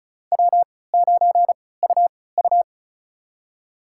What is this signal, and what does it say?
Morse code, heard as a keyed tone, about 34 wpm, spaced as for 20 wpm, 690 Hz. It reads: W9UU
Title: USA Call Signs - 3x Character Spacing - 35wpm